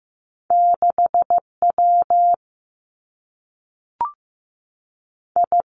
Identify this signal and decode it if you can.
Morse code, heard as a keyed tone, about 15 wpm, 700 Hz. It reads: 6 W I
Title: Two Challenging Letter-Number Combinations 15wpm